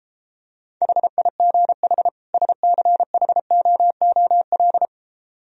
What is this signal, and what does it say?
Morse code, heard as a keyed tone, about 33 wpm, 700 Hz. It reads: HIGH SCHOOL